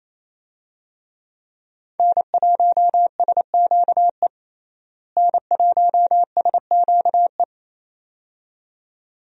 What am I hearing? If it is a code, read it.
Morse code, heard as a keyed tone, about 28 wpm, 700 Hz. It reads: N1SQE N1SQE
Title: USA Call Signs - 5 Characters - Repeated 2x - 28wpm